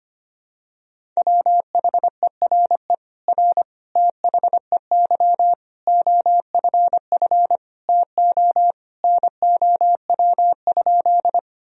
Morse code, heard as a keyed tone, about 25 wpm, 700 Hz. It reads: WHERE R THEY OFF TO NOW?
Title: Sentences from Top 200 Words and Abbreviations 25wpm